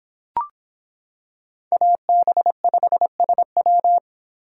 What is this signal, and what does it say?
Morse code, heard as a keyed tone, about 25 wpm, 700 Hz. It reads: AB5SW